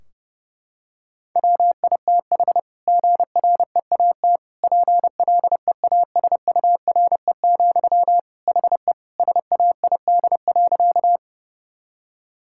Morse code, heard as a keyed tone, about 30 wpm, 700 Hz. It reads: WITH GREAT PLEASURE, HE SAID.
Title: Sentences from Top 1600 Words 30wpm